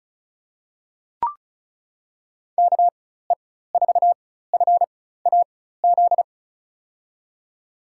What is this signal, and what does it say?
Morse code, heard as a keyed tone, about 36 wpm, spaced as for 16 wpm, 700 Hz. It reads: KE4FAZ
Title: USA Call Signs - 6 Characters - 4x Character Spacing - 35wpm